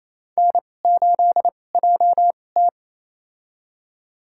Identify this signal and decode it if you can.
Morse code, heard as a keyed tone, about 29 wpm, spaced as for 21 wpm, 700 Hz. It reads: N8JT
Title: USA Call Signs - 4 Characters - 2x Character Spacing - 28wpm